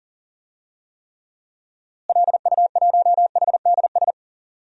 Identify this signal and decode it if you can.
Morse code, heard as a keyed tone, about 40 wpm, 700 Hz. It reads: LU1HDS
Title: DXCC-100 Call Signs - 6 Characters - Repeated 3x - 40wpm